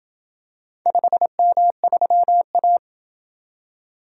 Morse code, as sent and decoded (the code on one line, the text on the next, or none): ..... -- ...-- .-
5M3A